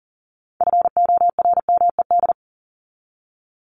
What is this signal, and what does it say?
Morse code, heard as a keyed tone, about 40 wpm, 700 Hz. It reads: FORMED
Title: ED Suffix - Top 100 Words with Base Words 40wpm